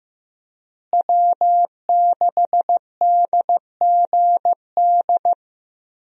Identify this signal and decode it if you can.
Morse code, heard as a keyed tone, about 15 wpm, 700 Hz. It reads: W6DGD
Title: USA Call Signs - 5 Characters - Repeated 3x - 15wpm